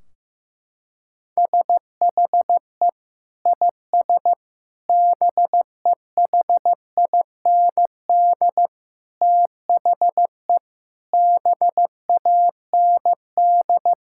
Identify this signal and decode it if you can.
Morse code, heard as a keyed tone, about 15 wpm, 700 Hz. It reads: SHE IS BEHIND THE BAND
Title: Sentences from Top 1600 Words 15wpm